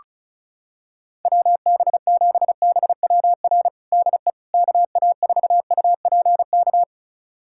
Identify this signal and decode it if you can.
Morse code, heard as a keyed tone, about 34 wpm, 700 Hz. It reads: WB7BWR DE KA4UPK
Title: USA Callsign DE Callsign 35wpm